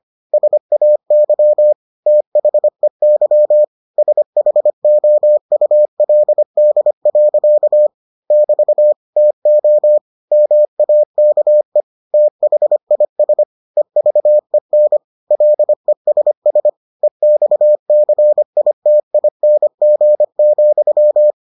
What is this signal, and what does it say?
Morse code, heard as a keyed tone, about 25 wpm, 600 Hz. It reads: SAY THEY SHOULD. <BT> TO MAKE THIS EVEN LESS EXCITING,